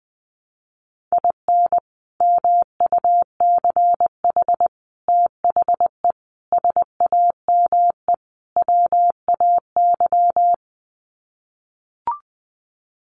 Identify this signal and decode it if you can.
Morse code, heard as a keyed tone, about 20 wpm, 700 Hz. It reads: IN MUCH THE SAME WAY
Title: Sets of 5 Words Encore 20wpm